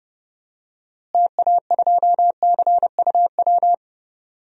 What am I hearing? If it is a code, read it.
Morse code, heard as a keyed tone, about 30 wpm, 700 Hz. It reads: TA2CUW
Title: DXCC-100 Call Signs - 6 Characters - Repeated 2x - 30wpm